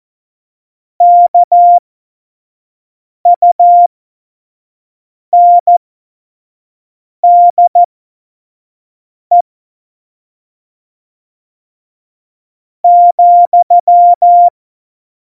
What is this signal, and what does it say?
Morse code, heard as a keyed tone, about 14 wpm, spaced as for 4 wpm, 700 Hz. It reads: KUNDE,
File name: Morsekurs / Witze / Witz 15. Januar 2019, 25 F70